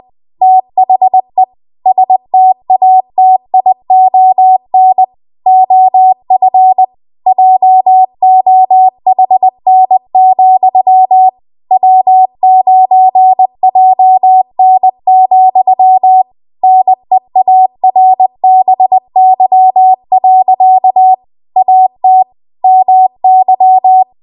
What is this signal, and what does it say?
Morse code, heard as a keyed tone, about 20 wpm, 750 Hz. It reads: THE STATION OF JOHN, W9JN, NEARBY. AT MY
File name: Morse / 20WPM